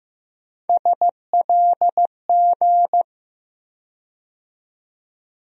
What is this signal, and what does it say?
Morse code, heard as a keyed tone, about 15 wpm, 700 Hz. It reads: SLG
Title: USA Call Sign Postfixes - 3 Letters 15wpm